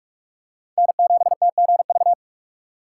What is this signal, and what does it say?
Morse code, heard as a keyed tone, about 46 wpm, 700 Hz. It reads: N7TGV
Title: USA Call Signs - 5 Characters - 4x Character Spacing - 45wpm